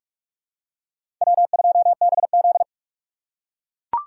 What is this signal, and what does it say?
Morse code, heard as a keyed tone, about 42 wpm, 700 Hz. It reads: W2BZ
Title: USA Call Signs - 4 Characters - 4x Character Spacing - 45wpm